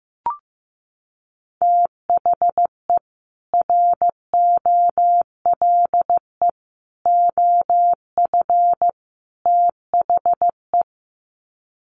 Morse code, heard as a keyed tone, about 14 wpm, 700 Hz. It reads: THE ROLE OF THE
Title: Sets of 4 Words 15wpm